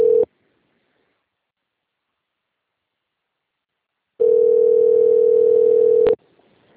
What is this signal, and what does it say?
Ringback tone (North America), 440 + 480 Hz.